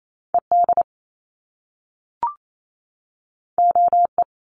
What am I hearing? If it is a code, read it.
Morse code, heard as a keyed tone, about 28 wpm, 700 Hz. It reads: ED OE
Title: Top Two Letter Combinations in Words 28wpm